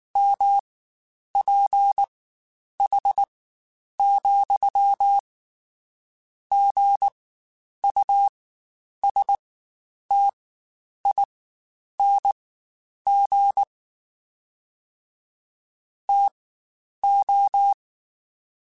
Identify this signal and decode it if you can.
Morse code, heard as a keyed tone, about 19 wpm, spaced as for 9 wpm, 800 Hz. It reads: MPH,GUSTING TO
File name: Spaced Characters / Spaced Characters-mp3s / Weather 4 21 sp @18